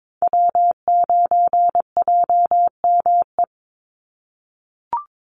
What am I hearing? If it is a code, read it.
Morse code, heard as a keyed tone, about 22 wpm, 700 Hz. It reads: W9JME
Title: USA Call Signs 22wpm